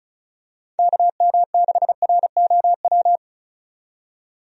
Morse code, heard as a keyed tone, about 36 wpm, 700 Hz. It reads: KM6ROW
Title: USA Call Signs - 6 Characters - 35wpm